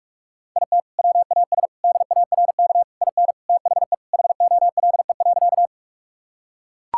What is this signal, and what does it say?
Morse code, heard as a keyed tone, about 41 wpm, 700 Hz. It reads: IT WAS DARK IN THE HOLE.
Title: Sentences from Top 1600 Words 45wpm